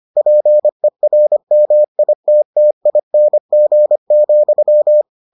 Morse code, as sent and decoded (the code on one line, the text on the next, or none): .--. .-. -- .. - - .. -. --. --..--
PERMITTING,